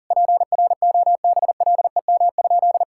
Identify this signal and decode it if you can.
Morse code, heard as a keyed tone, about 37 wpm, 700 Hz. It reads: PROBLEM?
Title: Sentences from Top 1000 Words - Review Part 6 - 40wpm